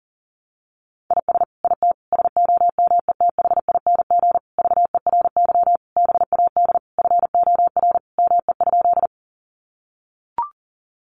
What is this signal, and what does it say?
Morse code, heard as a keyed tone, about 40 wpm, 700 Hz. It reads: IS IT SOMETHING VERY BAD FOR ME?
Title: Sentences from Top 700 Words 40wpm